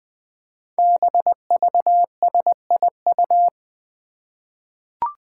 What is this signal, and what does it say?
Morse code, heard as a keyed tone, about 20 wpm, 700 Hz. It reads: BVSIU
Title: Five Challenging Letter-Number Combinations 20wpm